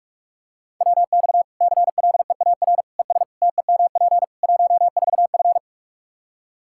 Morse code, heard as a keyed tone, about 41 wpm, 700 Hz. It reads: WX CLEAR ES TEMP 14F
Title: K1USN SST QSO Practice 45wpm